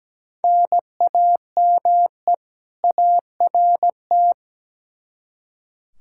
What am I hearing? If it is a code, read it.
Morse code, heard as a keyed tone, about 17 wpm, 700 Hz. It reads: NAME ART
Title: K1USN SST QSO Practice 17wpm